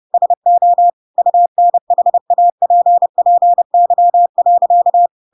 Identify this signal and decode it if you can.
Morse code, heard as a keyed tone, about 30 wpm, 700 Hz. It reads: SO UNHAPPY.
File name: Pinocchio / Pinocchio ch8 30 30 ik1hge